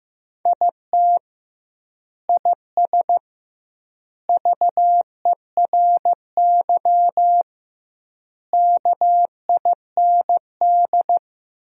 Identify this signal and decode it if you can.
Morse code, heard as a keyed tone, about 15 wpm, 700 Hz. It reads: IT IS VERY KIND
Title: 2x Word-Spacing - Sentences from Top 200 Words - 15wpm